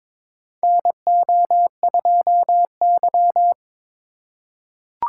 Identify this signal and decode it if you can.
Morse code, heard as a keyed tone, about 22 wpm, 700 Hz. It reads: NO2Y
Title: USA Call Signs - 4x Character Spacing - 22wpm